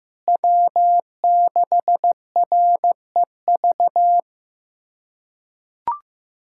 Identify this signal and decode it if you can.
Morse code, heard as a keyed tone, about 15 wpm, 700 Hz. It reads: W6REV
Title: USA Callsign DE Callsign 15wpm